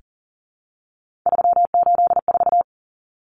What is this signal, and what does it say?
Morse code, heard as a keyed tone, about 40 wpm, 700 Hz. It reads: 384
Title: Common Cut Numbers - 3-Digit 40wpm